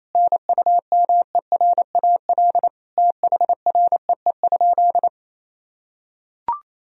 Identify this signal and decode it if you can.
Morse code, heard as a keyed tone, about 28 wpm, 700 Hz. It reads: NUMERAL THREE?
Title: Sentences from Top 1000 Words - Review Part 2 - 28wpm